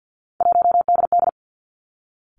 Morse code, heard as a keyed tone, about 50 wpm, 700 Hz. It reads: 1DD
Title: Three Challenging Letter-Number Combinations 50wpm